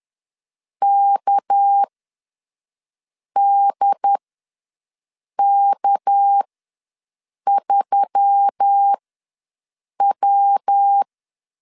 Morse code, heard as a keyed tone, about 10 wpm, spaced as for 5 wpm, 800 Hz. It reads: KDK3W
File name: LEZIONI / 6 PKX0 / 6 PK ws 10wpm 3745